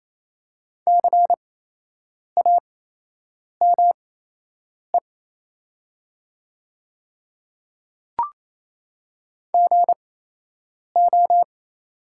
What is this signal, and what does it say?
Morse code, heard as a keyed tone, about 28 wpm, spaced as for 8 wpm, 690 Hz. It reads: CAME GO
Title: 8x Character Spacing - Top 200 Words - 28wpm